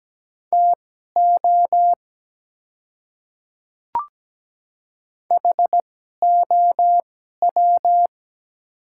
Morse code, heard as a keyed tone, about 17 wpm, spaced as for 12 wpm, 700 Hz. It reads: TO HOW
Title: 2x Character Spacing - Top 100 Words - 17wpm